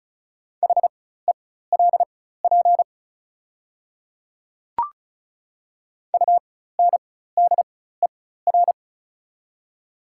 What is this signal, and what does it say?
Morse code, heard as a keyed tone, about 34 wpm, spaced as for 16 wpm, 700 Hz. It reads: HELP UNDER
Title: 4x Character Spacing - Top 300 Words - 35wpm